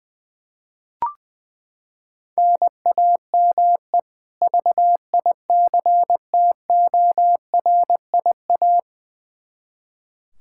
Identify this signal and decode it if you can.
Morse code, heard as a keyed tone, about 20 wpm, 700 Hz. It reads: NAME VICTORIA
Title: Practice QSO - Exchange 1 - Beginning - 20wpm